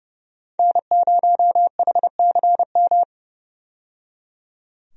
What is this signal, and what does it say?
Morse code, heard as a keyed tone, about 32 wpm, 700 Hz. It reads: N0HCM
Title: USA Call Signs - 5 Characters - 30wpm